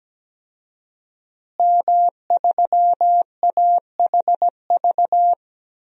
Morse code, heard as a keyed tone, about 17 wpm, 700 Hz. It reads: M3AHV
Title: DXCC-100 Call Signs - 5 Characters - Repeated 2x - 17wpm